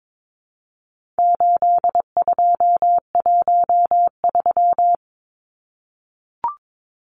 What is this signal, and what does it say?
Morse code, heard as a keyed tone, about 22 wpm, 700 Hz. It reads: 8213
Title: Numbers - 4-Digit 22wpm